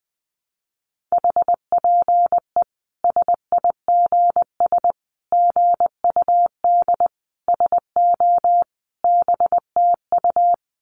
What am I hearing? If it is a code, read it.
Morse code, heard as a keyed tone, about 20 wpm, 700 Hz. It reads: HPE SIGS GUD SO BTU
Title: Common QSO Phrases 20wpm